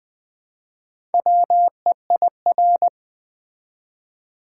Morse code, heard as a keyed tone, about 20 wpm, 700 Hz. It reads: WEIR
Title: Build A Word - 1 Letter At A Time - 5-Letter Words - 20wpm